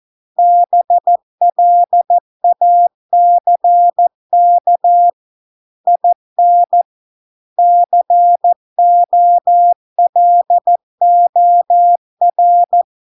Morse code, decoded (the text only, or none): BLACK IN COLOR